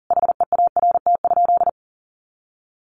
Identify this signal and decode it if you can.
Morse code, heard as a keyed tone, about 40 wpm, 700 Hz. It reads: HEART?
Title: Sentences from Top 1400 Words 40wpm